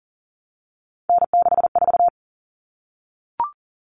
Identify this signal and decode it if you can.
Morse code, heard as a keyed tone, about 40 wpm, 700 Hz. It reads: N64
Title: Common Cut Numbers - 3-Digit 40wpm